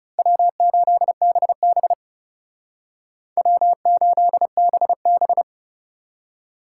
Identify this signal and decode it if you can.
Morse code, heard as a keyed tone, about 34 wpm, 700 Hz. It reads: W8BB W8BB
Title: Speed Racing - USA Call Signs 30wpm